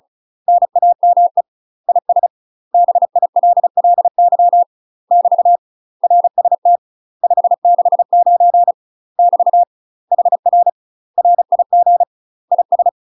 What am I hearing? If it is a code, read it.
Morse code, heard as a keyed tone, about 35 wpm, 700 Hz. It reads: NAME IS BILLY <BT> RST 569 <BT> HR RIG IS
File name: UK MASTER QSOs TRAINING / 35WPM / UK MASTER QSOs 0007